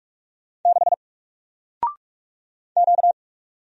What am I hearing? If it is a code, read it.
Morse code, heard as a keyed tone, about 44 wpm, 700 Hz. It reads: 6 Q
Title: Single Letter-Number Alt 45wpm